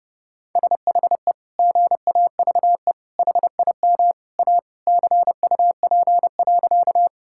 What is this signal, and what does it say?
Morse code, heard as a keyed tone, about 32 wpm, 700 Hz. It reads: SHE GAVE HIM A CUP.
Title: Sentences from Top 1600 Words 30wpm